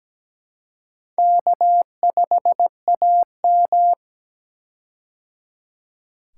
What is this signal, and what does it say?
Morse code, heard as a keyed tone, about 17 wpm, 700 Hz. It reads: K5AM
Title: K1USN SST QSO Practice - 17wpm